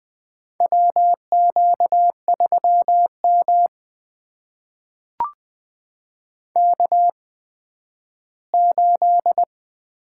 Morse code, heard as a keyed tone, about 21 wpm, spaced as for 12 wpm, 700 Hz. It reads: WQ3M K8
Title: USA Call Signs - 4 Characters - 8x Character Spacing - 20wpm